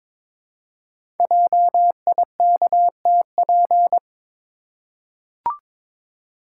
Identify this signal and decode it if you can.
Morse code, heard as a keyed tone, about 22 wpm, 700 Hz. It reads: JIKTP
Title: Code Groups - 5 Letters - 22wpm